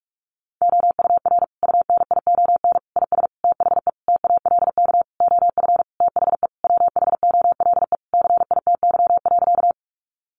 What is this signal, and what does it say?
Morse code, heard as a keyed tone, about 46 wpm, 700 Hz. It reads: OUR UNION IS THE TALK OF THE WHOLE CITY.
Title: Sentences from Top 1400 Words 45wpm